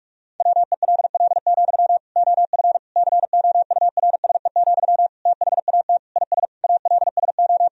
Morse code, heard as a keyed tone, about 46 wpm, 700 Hz. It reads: WELL, OF COURSE, THAT IS ALSO
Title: Sentences from Top 1000 Words - Review Part 1 - 45wpm